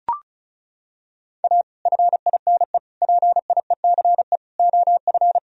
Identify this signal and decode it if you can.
Morse code, heard as a keyed tone, about 36 wpm, 700 Hz. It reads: A FINE PIECE OF
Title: Sentences from Top 1000 Words - Review Part 2 - 35wpm